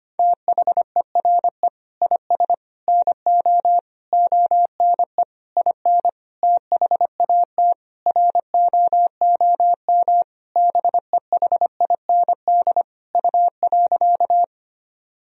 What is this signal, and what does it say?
Morse code, heard as a keyed tone, about 25 wpm, 700 Hz. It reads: THERE IS NO ONE IN THAT ROOM BEHIND U.
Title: Sentences from Top 500 Words and Abbreviations 25wpm